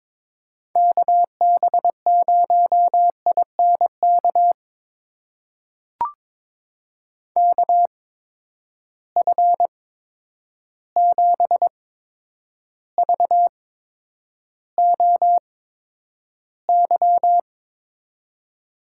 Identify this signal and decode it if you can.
Morse code, heard as a keyed tone, about 22 wpm, spaced as for 9 wpm, 700 Hz. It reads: KB0INK KF7VOY
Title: USA Call Signs - 6 Characters - 8x Character Spacing - 22wpm